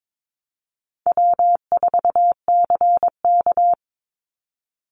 Morse code, heard as a keyed tone, about 22 wpm, 700 Hz. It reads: W4CK
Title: Call Signs - 22wpm